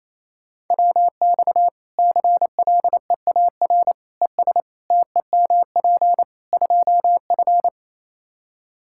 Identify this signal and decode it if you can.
Morse code, heard as a keyed tone, about 28 wpm, 700 Hz. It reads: WX CLEAR ES TEMP 2F
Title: Practice QSO - Exchange 2 - Info - 28wpm